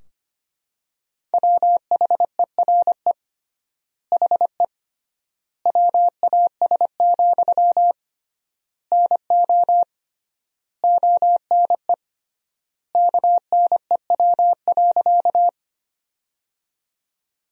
Morse code, heard as a keyed tone, about 25 wpm, 700 Hz. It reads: WHERE HE WAS, NO ONE KNEW.
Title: 3x Word-Spacing - Sentences from Top 400 Words - 25wpm